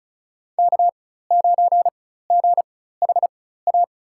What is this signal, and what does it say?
Morse code, heard as a keyed tone, about 36 wpm, spaced as for 16 wpm, 700 Hz. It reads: K9GHA
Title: USA Call Signs - 5 Characters - 4x Character Spacing - 35wpm